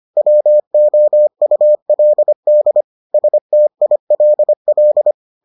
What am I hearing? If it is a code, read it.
Morse code, heard as a keyed tone, about 25 wpm, 600 Hz. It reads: WOULD STILL